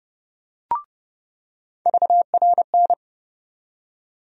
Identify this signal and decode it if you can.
Morse code, heard as a keyed tone, about 30 wpm, 700 Hz. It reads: VRN
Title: USA Call Sign Postfixes - 3 Letters 30wpm